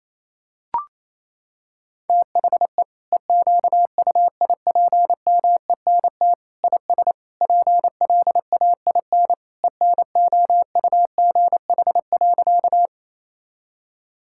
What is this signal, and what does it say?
Morse code, heard as a keyed tone, about 28 wpm, 700 Hz. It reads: THE EQUIPMENT IS PLAIN ENOUGH.
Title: Sentences from Top 1000 Words - Review Part 5 - 28wpm